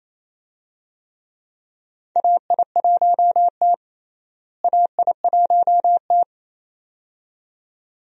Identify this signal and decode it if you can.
Morse code, heard as a keyed tone, about 28 wpm, 700 Hz. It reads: AI1T AI1T
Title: USA Call Signs - 4 Characters - Repeated 2x - 28wpm